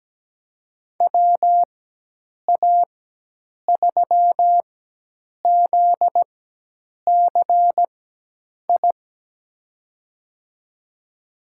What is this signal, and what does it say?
Morse code, heard as a keyed tone, about 17 wpm, spaced as for 8 wpm, 700 Hz. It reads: WA3ZCI
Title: USA Call Signs - 6 Characters - 4x Character Spacing - 17wpm